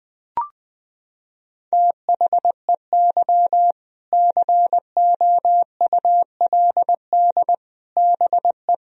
Morse code, heard as a keyed tone, about 21 wpm, 700 Hz. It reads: THEY COULD BE
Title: Sentences from Top 1000 Words - Review Part 4 - 20wpm